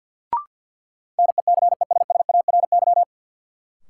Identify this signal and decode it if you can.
Morse code, heard as a keyed tone, about 48 wpm, 700 Hz. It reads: NECESSARY
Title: Sentences from Top 800 Words - New Word Drill - 50wpm